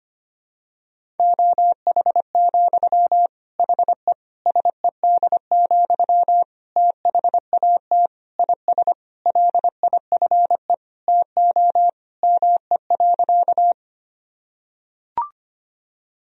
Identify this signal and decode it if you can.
Morse code, heard as a keyed tone, about 25 wpm, 700 Hz. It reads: OH, HE SED, THAT IS LIFE TO ME.
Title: Sentences from Top 500 Words and Abbreviations 25wpm